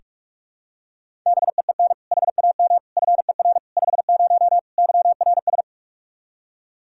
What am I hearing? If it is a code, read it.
Morse code, heard as a keyed tone, about 41 wpm, 700 Hz. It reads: BEEN HAM FER 50 YRS
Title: K1USN SST QSO Practice 45wpm